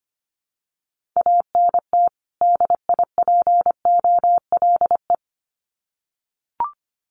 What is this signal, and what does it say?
Morse code, heard as a keyed tone, about 25 wpm, 700 Hz. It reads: ANT DIPOLE
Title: Common QSO Phrases 25wpm